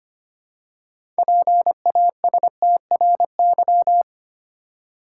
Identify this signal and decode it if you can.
Morse code, heard as a keyed tone, about 25 wpm, 690 Hz. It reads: PASTRY
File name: practice / 25wpm / Easy Words with Hints - 25wpm